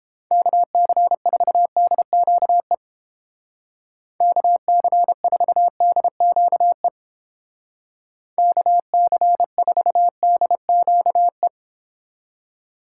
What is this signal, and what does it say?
Morse code, heard as a keyed tone, about 29 wpm, 700 Hz. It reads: KC4DQE KC4DQE KC4DQE